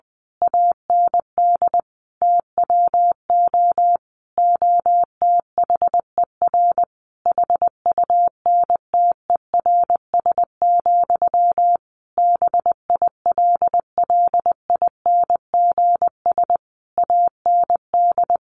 Morse code, heard as a keyed tone, about 19 wpm, 700 Hz. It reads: AND TWO OTHER HUNTERS, BILLINGS AND
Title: A Sound Of Thunder 20wpm